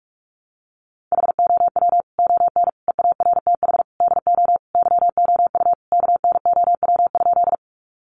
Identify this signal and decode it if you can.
Morse code, heard as a keyed tone, about 44 wpm, 700 Hz. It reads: HOW ON EARTH DO YOU KNOW?
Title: Sentences from Top 200 Words 45wpm